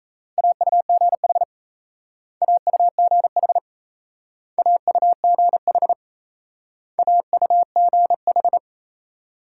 Morse code, heard as a keyed tone, about 38 wpm, 700 Hz. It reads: AUGH AUGH AUGH AUGH